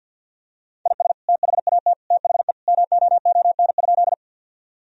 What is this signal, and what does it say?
Morse code, heard as a keyed tone, about 50 wpm, 700 Hz. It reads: IS THAT THE MOON?
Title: Sentences from Top 1000 Words - Review Part 3 - 50wpm